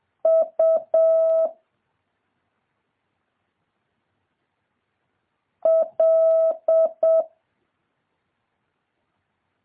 Morse code, heard as a keyed tone, about 7 wpm, 650 Hz. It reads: U L
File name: jh8cbh / 2023cw